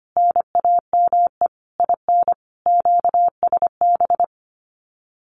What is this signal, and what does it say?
Morse code, heard as a keyed tone, about 25 wpm, 700 Hz. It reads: NAME IN QSB
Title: Common QSO Phrases 25wpm